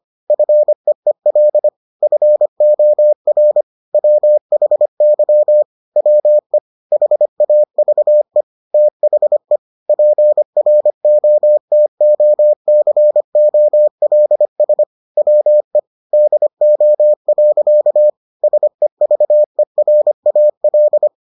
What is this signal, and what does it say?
Morse code, heard as a keyed tone, about 25 wpm, 600 Hz. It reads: FEEL FOR WHY WE HAVE THE PROTOCOLS WE DO. SEVERAL